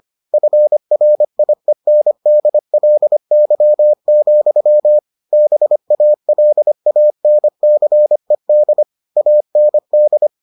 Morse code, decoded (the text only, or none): FRIENDLY, BALANCED AND